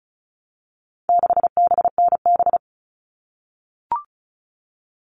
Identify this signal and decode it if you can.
Morse code, heard as a keyed tone, about 34 wpm, 700 Hz. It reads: 6BNB
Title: All Cut Numbers - 4-Digit 35wpm